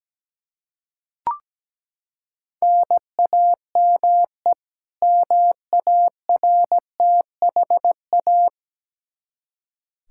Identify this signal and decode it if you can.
Morse code, heard as a keyed tone, about 17 wpm, 700 Hz. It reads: NAME MARTHA
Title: Practice QSO - Exchange 1 - Beginning - 17wpm